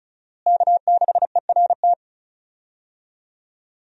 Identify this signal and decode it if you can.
Morse code, heard as a keyed tone, about 36 wpm, 700 Hz. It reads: K6ERT
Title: USA Call Signs - 5 Characters - Repeated 3x - 35wpm